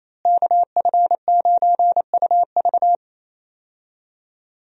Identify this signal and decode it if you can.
Morse code, heard as a keyed tone, about 28 wpm, 700 Hz. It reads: KF9UV
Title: USA Call Signs - 5 Characters - Repeated 3x - 28wpm